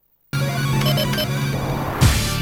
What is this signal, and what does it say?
There is no code.